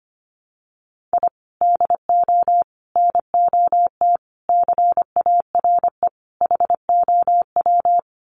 Morse code, heard as a keyed tone, about 25 wpm, 700 Hz. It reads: I DO NOT CARE HOW
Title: Sets of 5 Words Encore2 25wpm